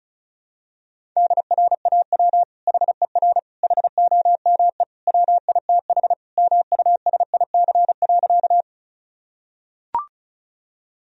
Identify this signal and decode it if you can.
Morse code, heard as a keyed tone, about 36 wpm, 700 Hz. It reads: DRAW HER HOME WITH MUSIC.
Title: Sentences from Top 1000 Words - Review Part 1 - 35wpm